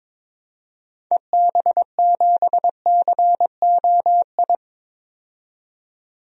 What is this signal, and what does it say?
Morse code, heard as a keyed tone, about 22 wpm, 700 Hz. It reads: EB7COI
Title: DXCC-100 Call Signs - 6 Characters - 22wpm